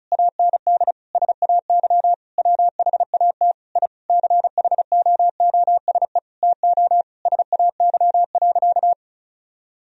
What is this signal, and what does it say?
Morse code, heard as a keyed tone, about 36 wpm, 700 Hz. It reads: AND SAY WHAT I CHOOSE TO SAY.
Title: Sentences from Top 1000 Words - Review Part 4 - 35wpm